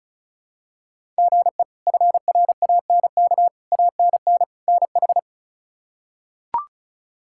Morse code, heard as a keyed tone, about 36 wpm, 700 Hz. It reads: GE FRANK ANN NH
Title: K1USN SST QSO Practice - 35wpm